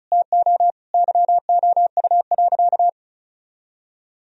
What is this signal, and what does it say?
Morse code, heard as a keyed tone, about 36 wpm, 700 Hz. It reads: TO YOU.
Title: Sentences from Top 1000 Words - Review Part 1 - 35wpm